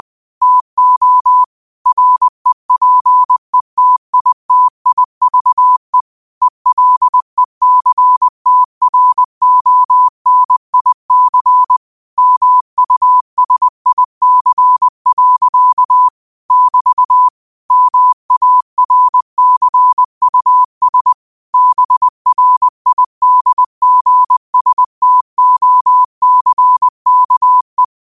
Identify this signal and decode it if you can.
Morse code, heard as a keyed tone, about 20 wpm, 1000 Hz. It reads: TO REPETITIVE ELECTRONIC MUSIC. <BT> MARCUS BRIDGSTOCKE
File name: qotdfiles / 20wpm